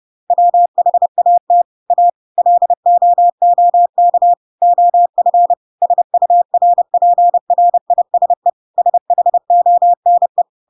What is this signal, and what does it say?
Morse code, heard as a keyed tone, about 30 wpm, 700 Hz. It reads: WHAT A LOOK OF SURPRISE SHONE
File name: Pinocchio / Pinocchio ch1 30 30 ik1hge